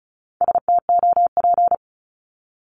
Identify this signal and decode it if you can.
Morse code, heard as a keyed tone, about 36 wpm, 700 Hz. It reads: STOP
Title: Multi-letter Phonemes 35wpm